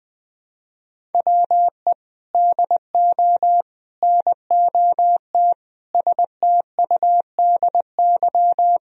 Morse code, heard as a keyed tone, about 20 wpm, 700 Hz. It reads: WE DO NOT STUDY
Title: Sentences from Top 1000 Words - Review Part 1 - 20wpm